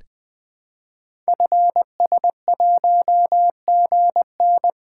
Morse code, heard as a keyed tone, about 20 wpm, 700 Hz. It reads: FS1GN
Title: Code Groups - 5 Letter:Numbers - 20wpm